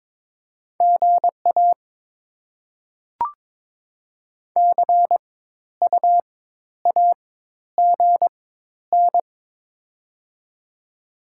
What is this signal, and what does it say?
Morse code, heard as a keyed tone, about 22 wpm, spaced as for 11 wpm, 690 Hz. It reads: GA CUAGN
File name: Top 20 QSO Elements - 4x Character Spacing - 22wpm